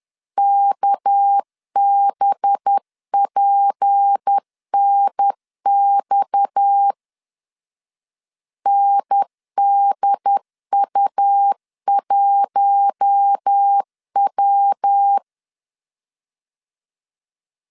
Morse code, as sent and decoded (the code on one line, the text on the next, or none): -.- -... .--. -. -..- / -. -.. ..- .---- .--
KBPNX NDU1W